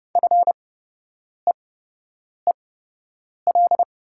Morse code, heard as a keyed tone, about 32 wpm, spaced as for 8 wpm, 700 Hz. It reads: FEEL
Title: 8x Character Spacing - Top 500 Words - 30wpm